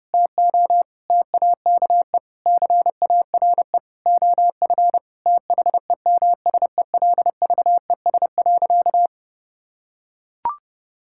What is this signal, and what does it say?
Morse code, heard as a keyed tone, about 32 wpm, 700 Hz. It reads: TO TAKE CARE OF THEMSELVES.
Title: Sentences from Top 1000 Words - Review Part 5 - 30wpm